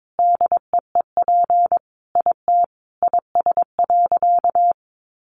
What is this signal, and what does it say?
Morse code, heard as a keyed tone, about 22 wpm, 700 Hz. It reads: DEEP IT IS.
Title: Sentences from Top 1100 Words 22wpm